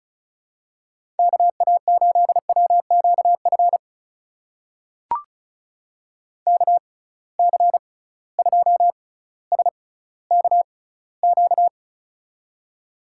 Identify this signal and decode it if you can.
Morse code, heard as a keyed tone, about 34 wpm, spaced as for 18 wpm, 700 Hz. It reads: KA8WQF KC2SKQ
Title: USA Call Signs - 6 Characters - 6x Character Spacing - 35wpm